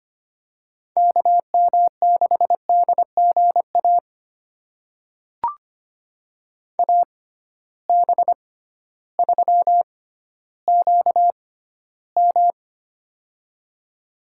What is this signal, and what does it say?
Morse code, heard as a keyed tone, about 25 wpm, spaced as for 13 wpm, 700 Hz. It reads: KM6DGA AB3QM
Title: USA Call Signs - 6x Character Spacing - 25wpm